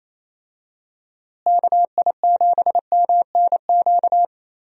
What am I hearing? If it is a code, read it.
Morse code, heard as a keyed tone, about 27 wpm, 700 Hz. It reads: KI7MNQ
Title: USA Call Signs - 6 Characters - Repeated 2x - 28wpm